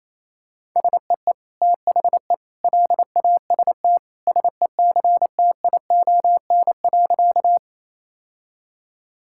Morse code, heard as a keyed tone, about 28 wpm, 700 Hz. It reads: SEE THE LAST SECTION.
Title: Sentences from Top 1000 Words - Review Part 5 - 28wpm